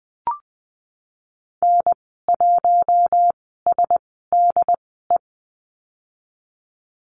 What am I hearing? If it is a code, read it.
Morse code, heard as a keyed tone, about 20 wpm, spaced as for 15 wpm, 700 Hz. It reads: N1SDE
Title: 076 Lesson 13g - Letter L - Call Signs 20wpm